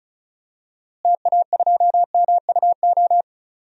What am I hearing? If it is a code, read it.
Morse code, heard as a keyed tone, about 34 wpm, 700 Hz. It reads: TA2MUO
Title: DXCC-100 Call Signs - 6 Characters - Repeated 3x - 35wpm